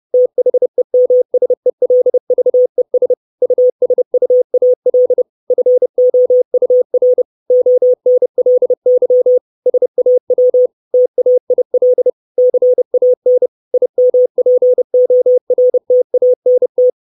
Morse code, heard as a keyed tone, about 30 wpm, 500 Hz. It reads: THEMSELVES USUAL FOUR ONLY SAW TAIL CAN IMPORTANT
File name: 632 words / 632 Words 30WPM